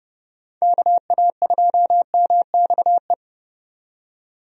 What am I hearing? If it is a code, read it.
Morse code, heard as a keyed tone, about 30 wpm, 700 Hz. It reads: KA2MXE
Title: USA Call Signs - 6 Characters - 30wpm